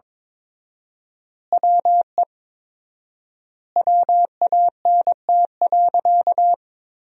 Morse code, heard as a keyed tone, about 22 wpm, 700 Hz. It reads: WE WANT.
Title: 4x Word-Spacing - Sentences from Top 200 Words - 22wpm